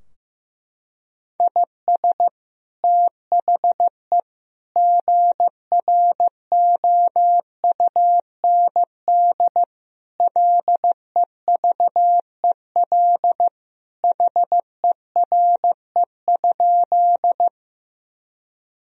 Morse code, heard as a keyed tone, about 15 wpm, 700 Hz. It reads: IS THE GROUND LEVEL HERE?
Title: Sentences from Top 1000 Words - Review Part 4 - 15wpm